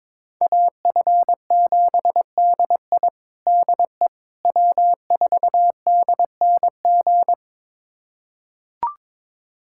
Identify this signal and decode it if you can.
Morse code, heard as a keyed tone, about 22 wpm, 700 Hz. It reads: AF7DI DE W4DNG
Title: USA Callsign DE Callsign - 5 Characters - 22wpm